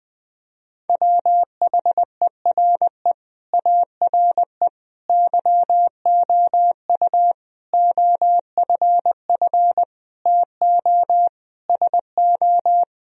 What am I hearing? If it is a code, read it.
Morse code, heard as a keyed tone, about 21 wpm, 700 Hz. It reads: WHERE ARE YOU OFF TO SO
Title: Sentences from Top 1000 Words - Review Part 2 - 20wpm